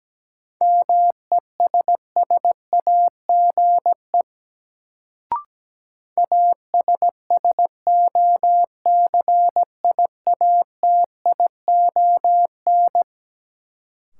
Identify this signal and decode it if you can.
Morse code, heard as a keyed tone, about 17 wpm, 700 Hz. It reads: MESSAGE ASSOCIATION
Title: Sentences from Top 1100 Words - New Word Drill - 17wpm